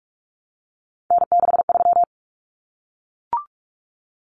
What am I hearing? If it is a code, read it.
Morse code, heard as a keyed tone, about 44 wpm, 700 Hz. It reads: N63
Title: Common Cut Numbers - 3-Digit 45wpm